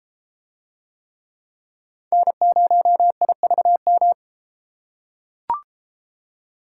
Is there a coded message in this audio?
Yes: Morse code from 2.1 to 4.1 s, heard as a keyed tone, about 32 wpm, 700 Hz. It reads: N0IVM